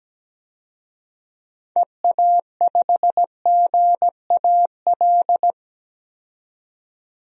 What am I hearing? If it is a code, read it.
Morse code, heard as a keyed tone, about 17 wpm, 700 Hz. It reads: EA5GAL
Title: DXCC-100 Call Signs - 6 Characters - 17wpm